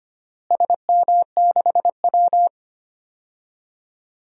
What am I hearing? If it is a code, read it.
Morse code, heard as a keyed tone, about 25 wpm, 700 Hz. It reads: SM6W
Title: DXCC-100 Call Signs - 4 Characters - Repeated 3x - 25wpm